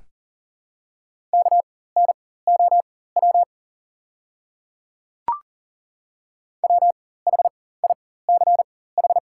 Morse code, heard as a keyed tone, about 38 wpm, spaced as for 18 wpm, 690 Hz. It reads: KNOW WHICH
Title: 4x Character Spacing - Top 200 Words - 40wpm